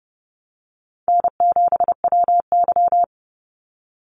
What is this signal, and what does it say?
Morse code, heard as a keyed tone, about 30 wpm, 700 Hz. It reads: N7WY
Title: Easy Call Signs - 30wpm